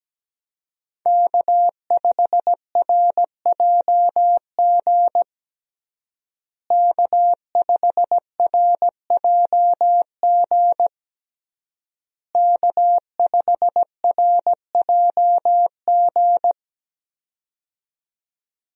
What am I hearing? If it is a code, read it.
Morse code, heard as a keyed tone, about 17 wpm, 700 Hz. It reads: K5RJG K5RJG K5RJG
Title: USA Call Signs - 5 Characters - Repeated 3x - 17wpm